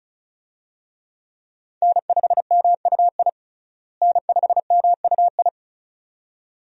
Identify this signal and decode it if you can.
Morse code, heard as a keyed tone, about 34 wpm, 700 Hz. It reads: N5MUI N5MUI
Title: USA Call Signs - 5 Characters - Repeated 2x - 35wpm